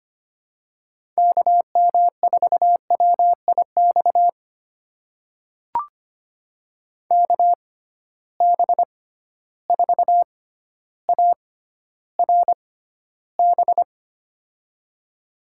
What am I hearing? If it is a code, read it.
Morse code, heard as a keyed tone, about 25 wpm, spaced as for 13 wpm, 700 Hz. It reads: KM4WIX KB4ARB
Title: USA Call Signs - 6 Characters - 6x Character Spacing - 25wpm